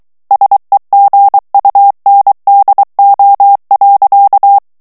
Morse code, heard as a keyed tone, about 23 wpm, 800 Hz. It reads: SEGUNDO.
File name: morse mp3 / ESPAGNOL TEXTE EN CLAIR 2 VITESSE 22 MOTS MINUTES